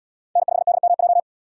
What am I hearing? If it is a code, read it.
Morse code, heard as a keyed tone, about 77 wpm, 700 Hz. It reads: N5DMJ